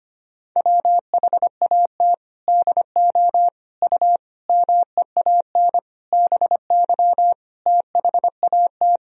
Morse code, heard as a keyed tone, about 25 wpm, 700 Hz. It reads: WHAT DO U MEAN BY THAT
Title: Sentences from Top 400 Words and Abbreviations 25wpm